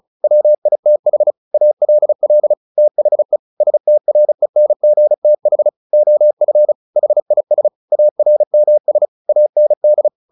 Morse code, heard as a keyed tone, about 34 wpm, 600 Hz. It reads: WITH ALL THE STRENGTH OF HIS ARMS AND
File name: the old man and the sea / the old man and the sea 1 35wpm